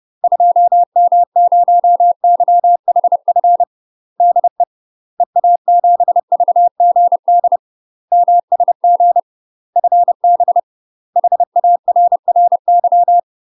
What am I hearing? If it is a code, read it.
Morse code, heard as a keyed tone, about 30 wpm, 700 Hz. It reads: 2M0YHF DE EA7VGD MSG FB HARRY
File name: UK MASTER QSOs TRAINING / 30WPM / UK MASTER QSOs 0039